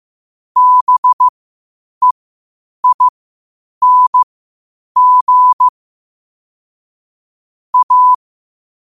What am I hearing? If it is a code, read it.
Morse code, heard as a keyed tone, about 15 wpm, spaced as for 8 wpm, 1000 Hz. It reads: BEING A
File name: qotdfiles / 8wpm